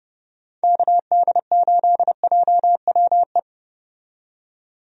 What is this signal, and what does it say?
Morse code, heard as a keyed tone, about 30 wpm, 700 Hz. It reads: KD8JWE
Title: USA Call Signs - 6 Characters - 30wpm